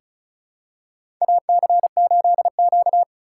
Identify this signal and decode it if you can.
Morse code, heard as a keyed tone, about 34 wpm, 700 Hz. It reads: AC8Q
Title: USA Call Signs - 4x Character Spacing - 35wpm